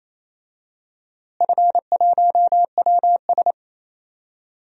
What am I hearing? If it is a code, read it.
Morse code, heard as a keyed tone, about 28 wpm, 700 Hz. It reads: F1WS
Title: DXCC-100 Call Signs - 4 Characters - Repeated 3x - 28wpm